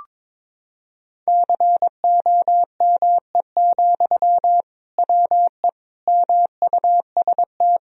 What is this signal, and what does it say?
Morse code, heard as a keyed tone, about 22 wpm, 700 Hz. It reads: COME, WE MUST
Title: Sentences from Top 1000 Words - Review Part 1 - 22wpm